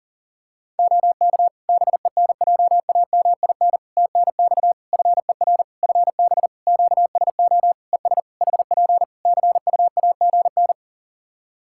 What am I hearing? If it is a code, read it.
Morse code, heard as a keyed tone, about 40 wpm, 700 Hz. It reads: OK BENJAMIN TNX FER FB QSO ES HP CUAGN
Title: Practice QSO - Exchange 4 - Ending - 40wpm